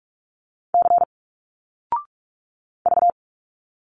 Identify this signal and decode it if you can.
Morse code, heard as a keyed tone, about 46 wpm, 700 Hz. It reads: C V
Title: Single Letters 45wpm